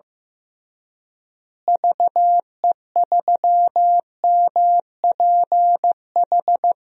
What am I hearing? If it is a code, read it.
Morse code, heard as a keyed tone, about 15 wpm, 700 Hz. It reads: VE3MPH
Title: DXCC-100 Call Signs - 6 Characters - Repeated 3x - 15wpm